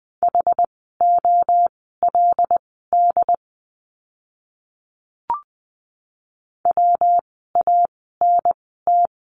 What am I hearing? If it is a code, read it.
Morse code, heard as a keyed tone, about 20 wpm, spaced as for 15 wpm, 700 Hz. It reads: HOLD WANT
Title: 146 Lesson 24f - Letter y - Words 20wpm